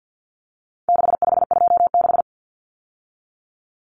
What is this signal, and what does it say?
Morse code, heard as a keyed tone, about 50 wpm, 700 Hz. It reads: 6526
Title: Numbers - 4-Digit 50wpm